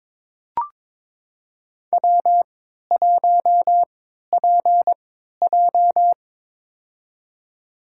Morse code, heard as a keyed tone, about 22 wpm, spaced as for 13 wpm, 700 Hz. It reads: W1PJ
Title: USA Call Signs - 4 Characters - 3x Character Spacing - 22wpm